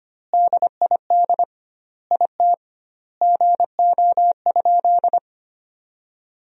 Morse code, heard as a keyed tone, about 25 wpm, 700 Hz. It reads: DID IT GO?
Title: 2x Word-Spacing - Sentences from Top 200 Words - 25wpm